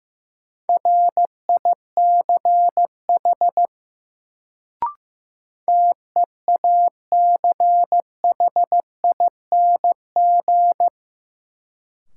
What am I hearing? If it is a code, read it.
Morse code, heard as a keyed tone, about 15 wpm, 700 Hz. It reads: RICH TEACHING
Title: Sentences from Top 1100 Words - New Word Drill Alt - 15wpm